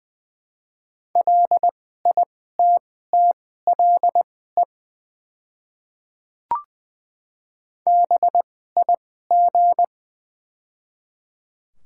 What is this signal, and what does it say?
Morse code, heard as a keyed tone, about 20 wpm, spaced as for 14 wpm, 700 Hz. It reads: LITTLE BIG
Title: 2x Character Spacing - Top 200 Words - 20wpm